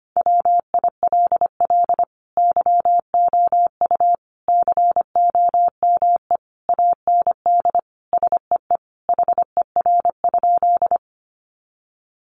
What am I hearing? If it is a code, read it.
Morse code, heard as a keyed tone, about 25 wpm, 700 Hz. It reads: WILL YOU COME AND SEE HER?
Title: Sentences from Top 100 Words 25wpm